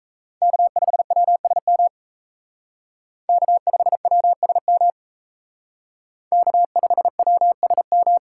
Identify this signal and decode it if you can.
Morse code, heard as a keyed tone, about 36 wpm, 700 Hz. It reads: K5WSM K5WSM K5WSM